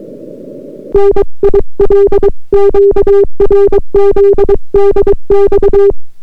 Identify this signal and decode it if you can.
Morse code, heard as a keyed tone, about 23 wpm, 400 Hz. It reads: NIL QRZ DX